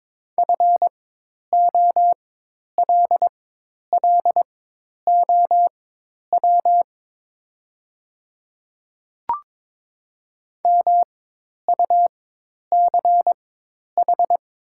Morse code, heard as a keyed tone, about 22 wpm, spaced as for 10 wpm, 700 Hz. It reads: FOLLOW MUCH
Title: 4x Character Spacing - Top 200 Words - 22wpm